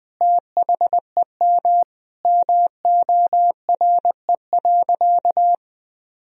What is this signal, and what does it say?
Morse code, heard as a keyed tone, about 20 wpm, 700 Hz. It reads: THEM MORE.
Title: Sentences from Top 1000 Words - Review Part 4 - 20wpm